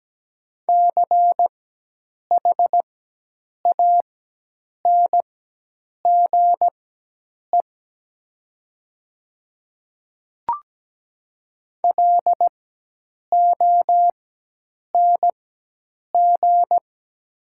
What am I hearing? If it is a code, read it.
Morse code, heard as a keyed tone, about 17 wpm, spaced as for 8 wpm, 700 Hz. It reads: CHANGE LONG
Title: 4x Character Spacing - Top 200 Words - 17wpm